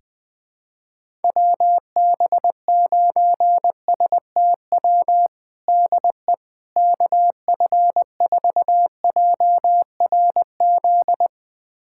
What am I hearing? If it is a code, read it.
Morse code, heard as a keyed tone, about 21 wpm, 700 Hz. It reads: WB9STW DE KF4JRZ
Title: USA Callsign DE Callsign 20wpm